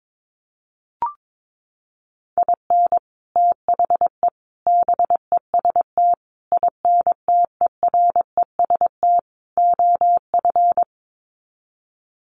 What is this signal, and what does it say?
Morse code, heard as a keyed tone, about 22 wpm, 700 Hz. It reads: IN THE BEST INTEREST OF
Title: Sets of 5 Words Encore 22wpm